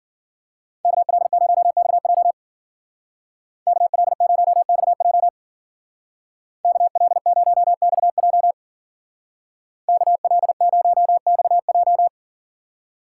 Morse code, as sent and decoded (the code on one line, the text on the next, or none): -.- .-.. ----- -..- .--- / -.- .-.. ----- -..- .--- / -.- .-.. ----- -..- .--- / -.- .-.. ----- -..- .---
KL0XJ KL0XJ KL0XJ KL0XJ